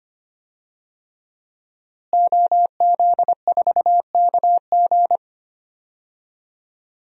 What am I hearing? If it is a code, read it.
Morse code, heard as a keyed tone, about 25 wpm, 700 Hz. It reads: OZ4KG
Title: DXCC-100 Call Signs - 5 Characters - Repeated 3x - 25wpm